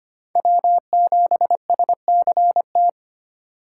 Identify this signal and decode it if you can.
Morse code, heard as a keyed tone, about 25 wpm, 700 Hz. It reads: W7SCT
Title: USA Call Signs - 5 Characters - 2x Character Spacing - 25wpm